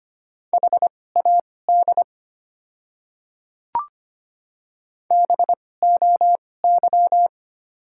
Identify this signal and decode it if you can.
Morse code, heard as a keyed tone, about 25 wpm, spaced as for 18 wpm, 700 Hz. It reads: HAD BOY
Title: 2x Character Spacing - Top 500 Words - 25wpm